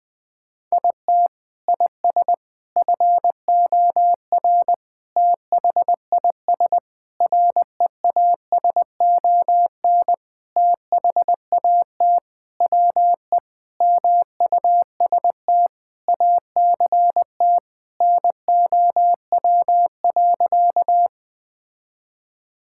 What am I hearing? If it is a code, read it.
Morse code, heard as a keyed tone, about 20 wpm, 700 Hz. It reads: IT IS FOR THIS REASON THAT WE MUST ACT NOW.
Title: Sentences with 6-Word Phrases - Spoken Beginning - 20wpm